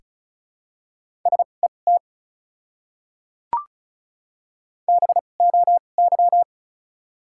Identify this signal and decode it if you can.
Morse code, heard as a keyed tone, about 36 wpm, spaced as for 25 wpm, 700 Hz. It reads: SET BOY
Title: 2x Character Spacing - Top 200 Words - 35wpm